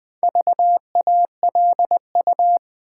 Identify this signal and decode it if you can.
Morse code, heard as a keyed tone, about 20 wpm, 700 Hz. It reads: VALU
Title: Build A Word - 4 Letters At A Time - 8-Letter Words - 20wpm